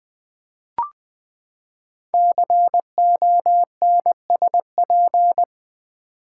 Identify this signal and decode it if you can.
Morse code, heard as a keyed tone, about 20 wpm, 700 Hz. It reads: CONSP
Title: Build A Word - 5 Letters At A Time - 10-Letter Words - 20wpm